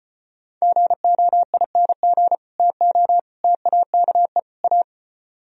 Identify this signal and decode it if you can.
Morse code, heard as a keyed tone, about 30 wpm, 700 Hz. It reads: GOING TO TAKE A
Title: Speed-Racing - Sets of 4 Words 25wpm